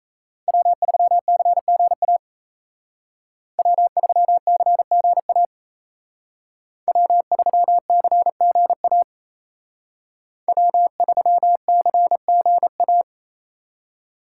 Morse code, heard as a keyed tone, about 36 wpm, 700 Hz. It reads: W3CGA W3CGA W3CGA W3CGA